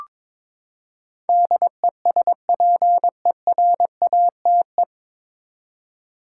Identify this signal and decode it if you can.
Morse code, heard as a keyed tone, about 22 wpm, 700 Hz. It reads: DESPERATE
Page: Morse Code Ninja practice 22wpm